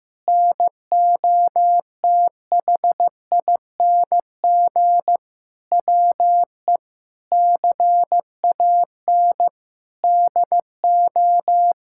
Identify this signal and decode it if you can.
Morse code, heard as a keyed tone, about 15 wpm, 700 Hz. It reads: NOTHING WE CAN DO
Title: Sentences from Top 1000 Words - Review Part 6 - 15wpm